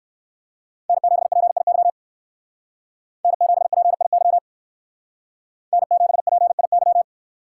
Morse code, heard as a keyed tone, about 58 wpm, 700 Hz. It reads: N7PIY N7PIY N7PIY